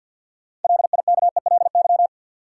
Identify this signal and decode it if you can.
Morse code, heard as a keyed tone, about 53 wpm, 700 Hz. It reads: LIKELY
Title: Sets of 2 Words - New Word Drill - 50wpm